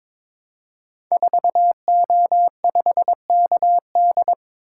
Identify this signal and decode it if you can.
Morse code, heard as a keyed tone, about 22 wpm, 700 Hz. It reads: 4O5KD
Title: Code Groups - 5 Letter:Numbers - 22wpm